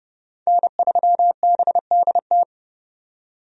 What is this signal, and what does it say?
Morse code, heard as a keyed tone, about 30 wpm, 700 Hz. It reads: N3BDT
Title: USA Call Signs - 5 Characters - 30wpm